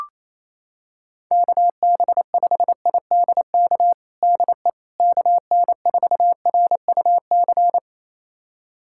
Morse code, heard as a keyed tone, about 28 wpm, 700 Hz. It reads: KB5IDK DE KN4RUC
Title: USA Callsign DE Callsign - 6 Characters - 28wpm